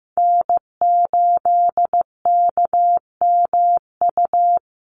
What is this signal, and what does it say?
Morse code, heard as a keyed tone, about 15 wpm, 700 Hz. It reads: N8KMU
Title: USA Call Signs Repeated 2x 15wpm